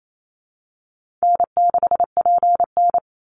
Morse code, heard as a keyed tone, about 28 wpm, 700 Hz. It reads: N6PN
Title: Easy Call Signs - 28wpm